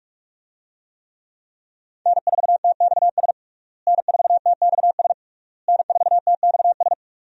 Morse code, heard as a keyed tone, about 41 wpm, 700 Hz. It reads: N4TXS N4TXS N4TXS
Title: USA Call Signs - 5 Characters - Repeated 3x - 45wpm